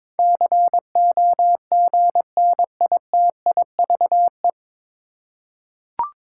Morse code, heard as a keyed tone, about 22 wpm, 700 Hz. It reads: COGNITIVE
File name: 9 Letter Words - First 3 Letters Given - 22wpm